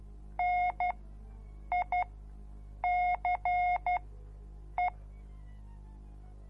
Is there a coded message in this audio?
Yes: Morse code from 0.4 to 4.9 s, heard as a keyed tone, about 12 wpm, spaced as for 7 wpm, 720 Hz. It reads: NICE